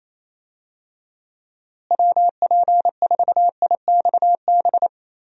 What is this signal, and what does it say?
Morse code, heard as a keyed tone, about 28 wpm, 700 Hz. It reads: WP4IXB
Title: USA Call Signs - 6 Characters - Repeated 2x - 28wpm